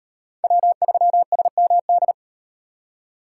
Morse code, heard as a keyed tone, about 38 wpm, 700 Hz. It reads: W3SMD